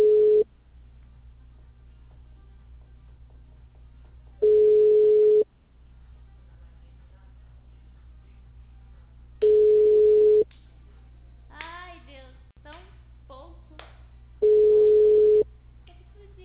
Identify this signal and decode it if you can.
Ringback tone (Europe, 425 Hz), with speech.